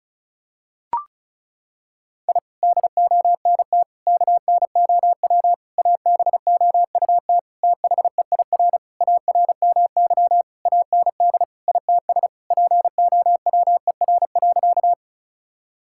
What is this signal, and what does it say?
Morse code, heard as a keyed tone, about 36 wpm, 700 Hz. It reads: I DONT KNOW ABOUT THEIR ARMY AND ITS POWER.
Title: Sentences from Top 1000 Words - Review Part 5 - 35wpm